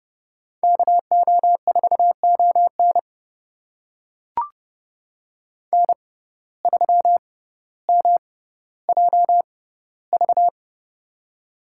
Morse code, heard as a keyed tone, about 30 wpm, spaced as for 15 wpm, 700 Hz. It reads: KO4ON N3MJV
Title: USA Call Signs - 5 Characters - 6x Character Spacing - 30wpm